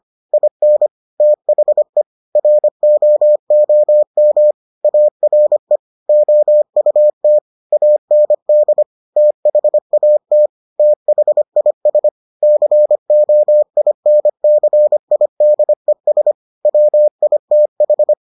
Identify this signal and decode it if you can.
Morse code, heard as a keyed tone, about 25 wpm, 600 Hz. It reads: IN THE ROOM ARE OUT AND THAT THIS COINCIDES WITH